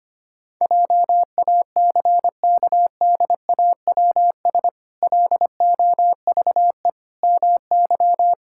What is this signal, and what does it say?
Morse code, heard as a keyed tone, about 25 wpm, 700 Hz. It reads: JACKDAWS LOVE MY
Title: Pangram Challenge v2 - 25wpm